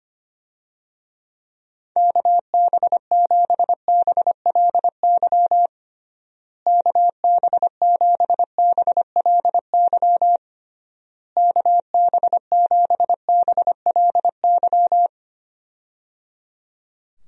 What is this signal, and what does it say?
Morse code, heard as a keyed tone, about 25 wpm, 700 Hz. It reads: KB7BLY KB7BLY KB7BLY
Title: USA Call Signs - 6 Characters - Repeated 3x - 25wpm